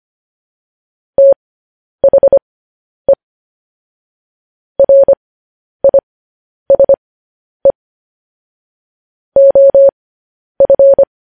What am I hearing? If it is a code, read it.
Morse code, heard as a keyed tone, about 25 wpm, spaced as for 10 wpm, 550 Hz. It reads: THE RISE OF